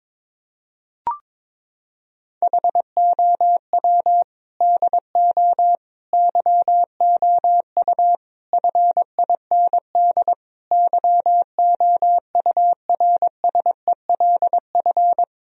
Morse code, heard as a keyed tone, about 22 wpm, 700 Hz. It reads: HOW DO YOU FIND YOURSELF
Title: Sentences from Top 1000 Words - Review Part 4 - 22wpm